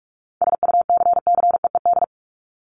Morse code, heard as a keyed tone, about 44 wpm, 700 Hz. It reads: SUCCEED
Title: Top 1501-1750 Words 45wpm